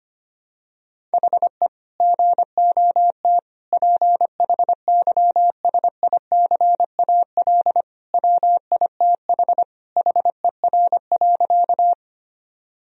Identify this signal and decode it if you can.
Morse code, heard as a keyed tone, about 25 wpm, 700 Hz. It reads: HE GOT PHYSICAL WITH HER.
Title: Sentences from Top 1000 Words - Review Part 4 - 25wpm